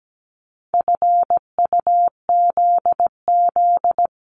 Morse code, heard as a keyed tone, about 17 wpm, 700 Hz. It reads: FUZZ
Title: Multi-letter Phonemes 17wpm